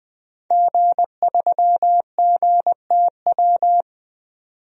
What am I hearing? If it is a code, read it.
Morse code, heard as a keyed tone, about 20 wpm, 700 Hz. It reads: G3GTW
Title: DXCC-100 Call Signs - 5 Characters - 20wpm